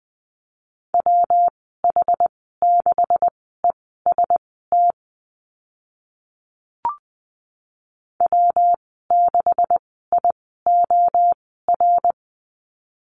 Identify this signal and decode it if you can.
Morse code, heard as a keyed tone, about 20 wpm, spaced as for 14 wpm, 700 Hz. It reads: WH6EST W6IOR
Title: 147 Lesson 24g - Letter y - Call Signs 20wpm